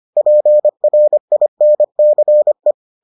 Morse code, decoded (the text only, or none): PRINCE